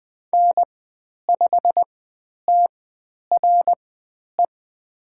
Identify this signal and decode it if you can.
Morse code, heard as a keyed tone, about 20 wpm, spaced as for 10 wpm, 700 Hz. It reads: N5TRE
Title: 109 Lesson 18g - Letter m - Call Signs 20wpm